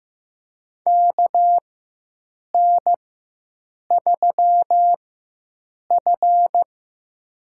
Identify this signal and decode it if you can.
Morse code, heard as a keyed tone, about 16 wpm, spaced as for 7 wpm, 700 Hz. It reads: KN3F
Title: USA Call Signs - 4 Characters - 4x Character Spacing - 15wpm